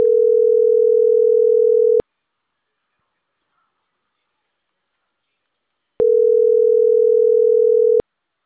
Ringback tone (North America), 440 + 480 Hz.